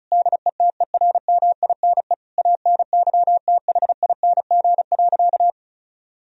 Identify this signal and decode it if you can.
Morse code, heard as a keyed tone, about 34 wpm, 700 Hz. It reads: DETERMINE ANYTHING.
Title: Sentences from Top 1000 Words - Review Part 4 - 35wpm